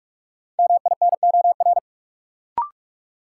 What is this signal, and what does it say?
Morse code, heard as a keyed tone, about 46 wpm, 700 Hz. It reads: MINOR
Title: Sentences from Top 1500 Words - New Word Drill - 45wpm